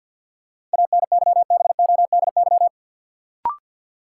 Morse code, heard as a keyed tone, about 53 wpm, 700 Hz. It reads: ANYBODY
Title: Sets of 5 Words - New Word Drill - 50wpm